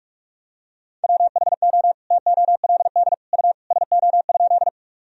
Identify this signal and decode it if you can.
Morse code, heard as a keyed tone, about 47 wpm, 700 Hz. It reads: WHO TOLD U SO?
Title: Sentences from Top 400 Words and Abbreviations 45wpm